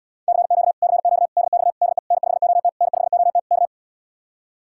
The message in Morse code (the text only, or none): CQ CQ DX DE N5OT N5OT K